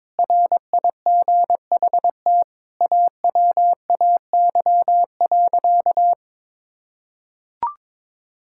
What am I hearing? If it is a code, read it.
Morse code, heard as a keyed tone, about 22 wpm, 700 Hz. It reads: RIGHT AWAY.
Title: Sentences from Top 1600 Words 22wpm